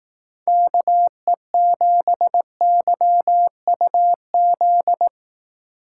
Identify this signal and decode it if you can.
Morse code, heard as a keyed tone, about 18 wpm, 700 Hz. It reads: KE7YUZ
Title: Speed Racing - USA 2x3 Call Signs 15wpm